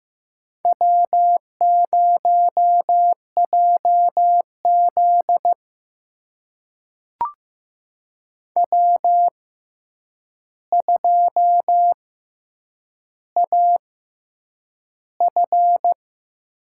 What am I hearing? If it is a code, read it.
Morse code, heard as a keyed tone, about 15 wpm, spaced as for 8 wpm, 700 Hz. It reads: W0JZ W2AF
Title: USA Call Signs - 4 Characters - 6x Character Spacing - 15wpm